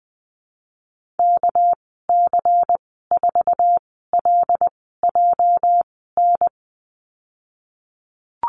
Morse code, heard as a keyed tone, about 19 wpm, spaced as for 14 wpm, 700 Hz. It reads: KC4LJN
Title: 207 Lesson 33g - Letter j - Call Signs 20wpm